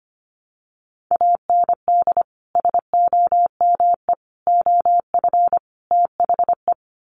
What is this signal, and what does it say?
Morse code, heard as a keyed tone, about 25 wpm, 700 Hz. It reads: AND SOME OF THE
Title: Sets of 4 Words 25wpm